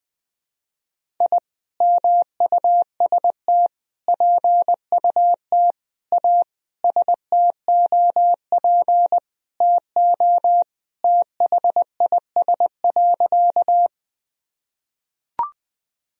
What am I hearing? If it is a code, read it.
Morse code, heard as a keyed tone, about 21 wpm, 700 Hz. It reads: I MUST PUT A STOP TO THIS.
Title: Sentences from Top 1000 Words - Review Part 1 - 20wpm